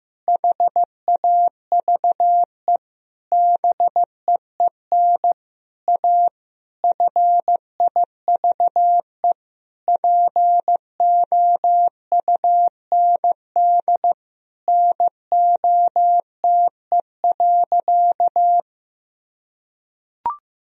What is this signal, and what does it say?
Morse code, heard as a keyed tone, about 15 wpm, 700 Hz. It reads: HAVE BEEN A FIVE POUND NOTE.
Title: Sentences from Top 1000 Words - Review Part 2 - 15wpm